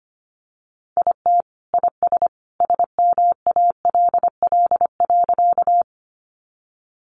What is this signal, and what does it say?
Morse code, heard as a keyed tone, about 25 wpm, 700 Hz. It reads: IT IS SMALL.
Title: Additional Easy Phrases 25wpm